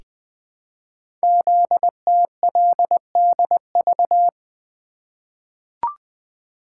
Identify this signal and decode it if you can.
Morse code, heard as a keyed tone, about 20 wpm, 700 Hz. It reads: ZTLDV
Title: Code Groups - 5 Letters - 20wpm